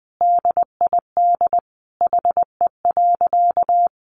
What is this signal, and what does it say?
Morse code, heard as a keyed tone, about 20 wpm, 700 Hz. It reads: DID HE.
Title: Sentences from Top 1500 Words 20wpm